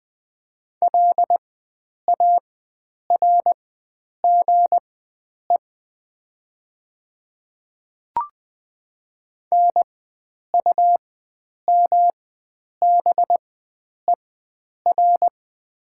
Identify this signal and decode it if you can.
Morse code, heard as a keyed tone, about 20 wpm, spaced as for 9 wpm, 700 Hz. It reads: LARGE NUMBER
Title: 4x Character Spacing - Top 300 Words - 20wpm